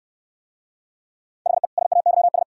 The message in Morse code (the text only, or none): HE SAYS